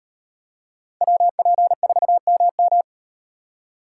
Morse code, heard as a keyed tone, about 38 wpm, 700 Hz. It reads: WP4MM